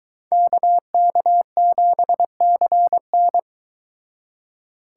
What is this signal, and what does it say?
Morse code, heard as a keyed tone, about 23 wpm, 700 Hz. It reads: KK7CN